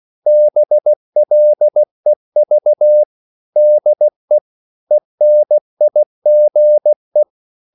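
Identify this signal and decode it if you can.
Morse code, heard as a keyed tone, about 16 wpm, 600 Hz. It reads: BLEV DE ENIGE
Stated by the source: Slow (QRS) Morse code at 16 wpm, from CW med Gnister.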